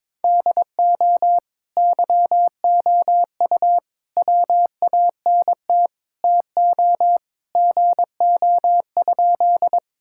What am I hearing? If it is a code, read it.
Morse code, heard as a keyed tone, about 22 wpm, 700 Hz. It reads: DO YOU WANT TO GO?
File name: practice / 22wpm / Sentences with 2-Word Phrases - 22wpm